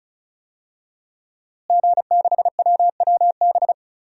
Morse code, heard as a keyed tone, about 34 wpm, 700 Hz. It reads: G6WWB
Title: DXCC-100 Call Signs - 5 Characters - Repeated 2x - 35wpm